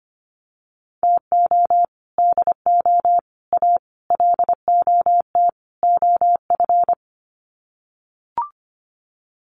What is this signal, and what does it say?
Morse code, heard as a keyed tone, about 25 wpm, 700 Hz. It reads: TO DO A LOT OF
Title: Sets of 5 Words Encore 25wpm